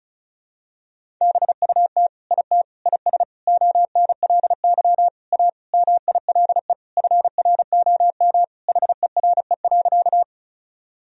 Morse code, heard as a keyed tone, about 34 wpm, 700 Hz. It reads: BUT IT IS ONLY A MILE FROM HERE.
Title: Sentences from Top 1000 Words - Review Part 1 - 35wpm